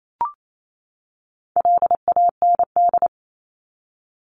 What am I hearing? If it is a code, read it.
Morse code, heard as a keyed tone, about 28 wpm, 700 Hz. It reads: LAND
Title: Top 500 Words 28wpm